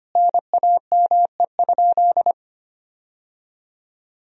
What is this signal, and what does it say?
Morse code, heard as a keyed tone, about 25 wpm, 700 Hz. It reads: NAME?
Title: 4x Word-Spacing - Sentences from Top 300 Words - 25wpm